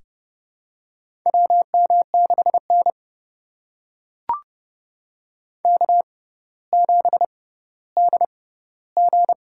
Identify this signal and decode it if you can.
Morse code, heard as a keyed tone, about 29 wpm, spaced as for 15 wpm, 700 Hz. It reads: WM6N K7DG
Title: USA Call Signs - 4 Characters - 6x Character Spacing - 30wpm